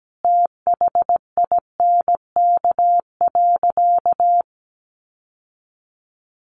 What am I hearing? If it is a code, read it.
Morse code, heard as a keyed tone, about 17 wpm, 700 Hz. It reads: THINK.
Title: Sentences from Top 600 Words 17wpm